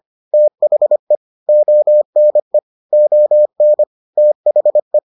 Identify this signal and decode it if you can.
Morse code, heard as a keyed tone, about 25 wpm, 600 Hz. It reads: THE ONE ON THE